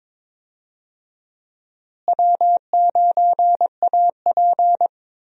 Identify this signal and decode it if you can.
Morse code, heard as a keyed tone, about 22 wpm, 700 Hz. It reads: W9AP
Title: USA Call Signs - 4 Characters - Repeated 3x - 22wpm